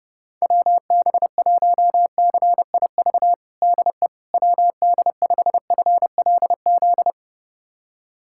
Morse code, heard as a keyed tone, about 30 wpm, 700 Hz. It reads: WB1CIV DE WD5FLZ
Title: USA Callsign DE Callsign - 6 Characters - 30wpm